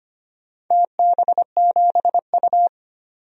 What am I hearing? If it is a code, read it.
Morse code, heard as a keyed tone, about 25 wpm, 700 Hz. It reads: TB7U
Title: DXCC-100 Call Signs - 4 Characters - 25wpm